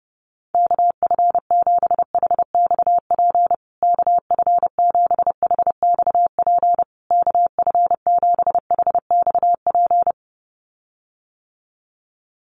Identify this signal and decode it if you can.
Morse code, heard as a keyed tone, about 30 wpm, 700 Hz. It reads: KF7HXP KF7HXP KF7HXP
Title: USA Call Signs Repeated 3x 30wpm